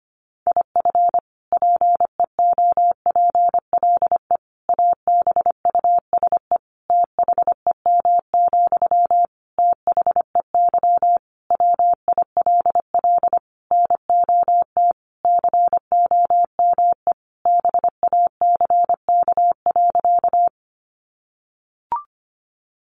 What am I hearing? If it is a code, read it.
Morse code, heard as a keyed tone, about 25 wpm, 700 Hz. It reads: IF PEOPLE ABUSE THEM, THEY WILL NOT COME BACK.
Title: Sentences from Top 1300 Words 25wpm